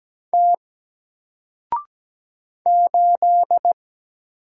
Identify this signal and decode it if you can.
Morse code, heard as a keyed tone, about 17 wpm, 700 Hz. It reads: T 8
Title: Single Character (Letter:Number:Punctuation:Prosign) 17wpm